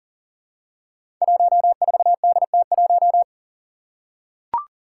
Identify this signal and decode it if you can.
Morse code, heard as a keyed tone, about 40 wpm, 700 Hz. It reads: 14DT1
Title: Five Challenging Letter-Number Combinations 40wpm